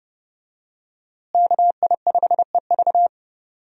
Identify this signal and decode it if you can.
Morse code, heard as a keyed tone, about 31 wpm, 700 Hz. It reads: KI5EV
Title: USA Call Signs - 5 Characters - Repeated 3x - 30wpm